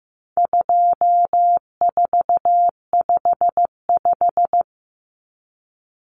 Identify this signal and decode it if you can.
Morse code, heard as a keyed tone, about 15 wpm, 700 Hz. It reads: 2455
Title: Common Cut Numbers - 4-Digit 15wpm